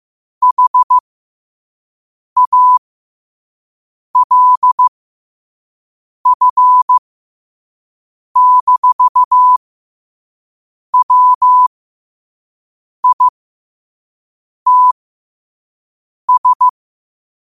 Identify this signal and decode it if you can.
Morse code, heard as a keyed tone, about 15 wpm, spaced as for 5 wpm, 1000 Hz. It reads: HALF-WITS
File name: qotdfiles / 5wpm